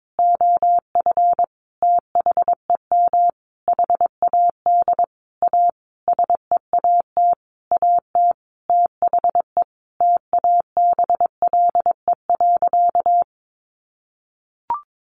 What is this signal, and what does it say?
Morse code, heard as a keyed tone, about 22 wpm, 700 Hz. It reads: OF THEM HAD A SEAT AT THE TABLE.
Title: Sentences from Top 1500 Words 22wpm